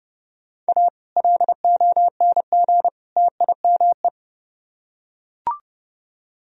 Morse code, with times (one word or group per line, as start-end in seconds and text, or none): 0.7-0.9 A
1.2-2.9 LONG
3.2-4.1 TIME